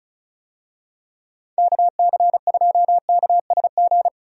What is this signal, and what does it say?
Morse code, heard as a keyed tone, about 34 wpm, 700 Hz. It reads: KC2KSG
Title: USA Call Signs - 6 Characters - 35wpm